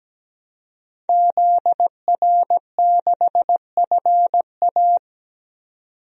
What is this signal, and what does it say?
Morse code, heard as a keyed tone, about 17 wpm, 700 Hz. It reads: ZR6FA
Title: DXCC-100 Call Signs - 5 Characters - 17wpm